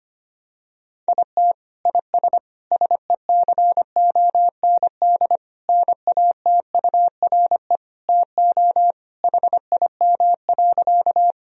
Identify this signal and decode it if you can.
Morse code, heard as a keyed tone, about 25 wpm, 700 Hz. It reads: IT IS SECOND NATURE TO HIM.
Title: Sentences from Top 1000 Words - Review Part 1 - 25wpm